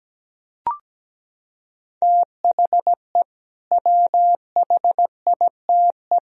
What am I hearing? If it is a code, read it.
Morse code, heard as a keyed tone, about 17 wpm, 700 Hz. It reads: THE WHITE
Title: Sentences from Top 1000 Words - Review Part 2 - 17wpm